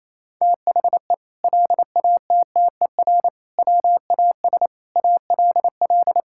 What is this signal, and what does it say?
Morse code, heard as a keyed tone, about 28 wpm, 700 Hz. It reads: THE LATTER WAS ALL
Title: Sentences from Top 1600 Words 28wpm